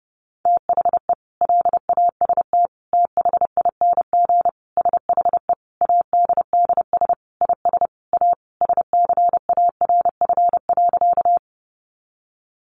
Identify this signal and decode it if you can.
Morse code, heard as a keyed tone, about 30 wpm, 700 Hz. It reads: THE LAST THING SHE ADDS IS A SCARF.
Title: Easy Phrases 30wpm